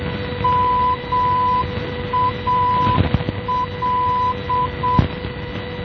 Morse code, heard as a keyed tone, about 7 wpm, 1000 Hz, with noise. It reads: MAL
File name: MAL NDB 1988